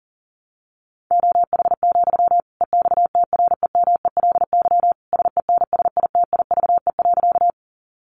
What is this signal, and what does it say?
Morse code, heard as a keyed tone, about 40 wpm, 700 Hz. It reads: OH, EXTREMELY SENSITIVE.
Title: Sentences from Top 1400 Words 40wpm